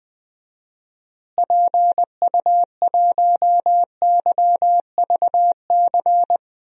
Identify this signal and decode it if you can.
Morse code, heard as a keyed tone, about 20 wpm, 700 Hz. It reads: PU1YVC
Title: DXCC-100 Call Signs - 6 Characters - Repeated 3x - 20wpm